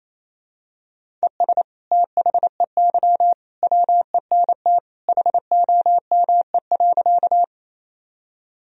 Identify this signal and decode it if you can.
Morse code, heard as a keyed tone, about 28 wpm, 700 Hz. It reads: ES THEY WENT HOME.
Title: Sentences from Top 200 Words and Abbreviations 28wpm